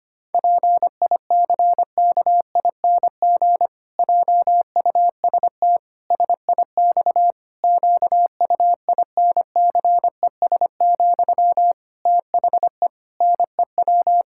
Morse code, heard as a keyed tone, about 25 wpm, 700 Hz. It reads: PICKING JUST SIX QUINCES, THE NEW
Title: Pangram Challenge v2 - 25wpm